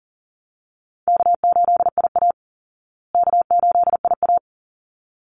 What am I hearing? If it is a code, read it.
Morse code, heard as a keyed tone, about 40 wpm, 700 Hz. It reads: K8IA K8IA
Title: Easy Call Signs Repeated 2x - 40wpm